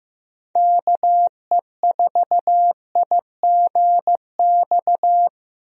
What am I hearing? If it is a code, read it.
Morse code, heard as a keyed tone, about 16 wpm, 700 Hz. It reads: KE4IGX
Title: USA Call Signs - 6 Characters - Repeated 3x - 15wpm